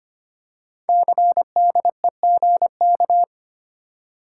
Morse code, heard as a keyed tone, about 25 wpm, 700 Hz. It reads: CDEGK
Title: Code Groups - 5 Letters - 25wpm